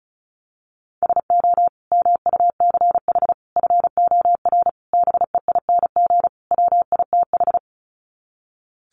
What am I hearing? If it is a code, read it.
Morse code, heard as a keyed tone, about 34 wpm, 700 Hz. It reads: SO MUCH FOR BEING WITH
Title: Sets of 5 Words Encore 35wpm